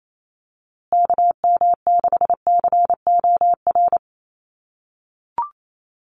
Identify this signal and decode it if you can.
Morse code, heard as a keyed tone, about 28 wpm, 700 Hz. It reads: KM6COR
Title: USA Call Signs Repeated 3x 28wpm